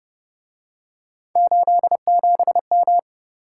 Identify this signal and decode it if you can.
Morse code, heard as a keyed tone, about 30 wpm, 700 Hz. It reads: 87M
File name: Speed-Racing - Three Letter-Number Combinations - 25wpm